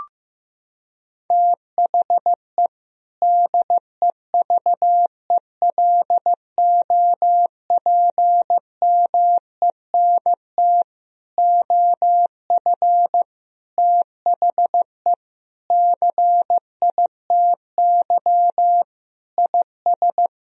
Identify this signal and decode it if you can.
Morse code, heard as a keyed tone, about 15 wpm, 700 Hz. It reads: THE DEVELOPMENT OF THE CITY IS
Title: Sentences from Top 1000 Words - Review Part 4 - 15wpm